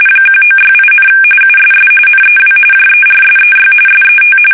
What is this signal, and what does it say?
RTTY, about 45.45 baud. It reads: 1968, USS NEW JERSEY SET
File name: fsk850 / battleshipnj